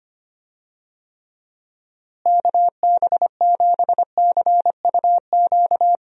Morse code, heard as a keyed tone, about 25 wpm, 700 Hz. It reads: KB7CUQ
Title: USA Call Signs - 6 Characters - Repeated 3x - 25wpm